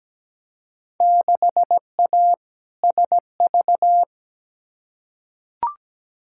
Morse code, heard as a keyed tone, about 17 wpm, 700 Hz. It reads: 6A SV
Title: Field Day Exchanges Encore - 17wpm